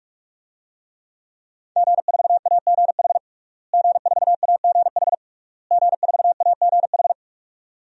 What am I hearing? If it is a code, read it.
Morse code, heard as a keyed tone, about 41 wpm, 700 Hz. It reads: G4AGH G4AGH G4AGH
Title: DXCC-100 Call Signs - 5 Characters - Repeated 3x - 45wpm